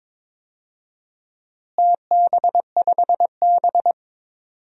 Morse code, heard as a keyed tone, about 22 wpm, 700 Hz. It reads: TB5B
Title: DXCC-100 Call Signs - 4 Characters - Repeated 3x - 22wpm